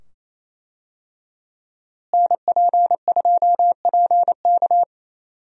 Morse code, heard as a keyed tone, about 27 wpm, 700 Hz. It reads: NP2PK
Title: USA Call Signs - 5 Characters - Repeated 2x - 28wpm